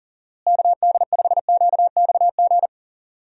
Morse code, heard as a keyed tone, about 38 wpm, 700 Hz. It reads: KD5QXG
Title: USA Call Signs - 6 Characters - Repeated 3x - 40wpm